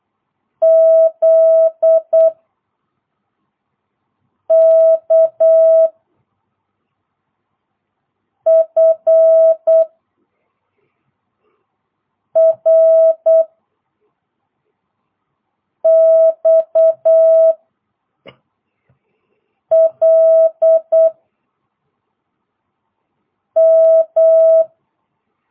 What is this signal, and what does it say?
Morse code, heard as a keyed tone, about 8 wpm, 650 Hz. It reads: Z K F R X L M